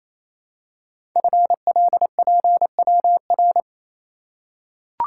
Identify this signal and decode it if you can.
Morse code, heard as a keyed tone, about 27 wpm, 700 Hz. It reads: FLPWR
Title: Code Groups - 5 Letters - 28wpm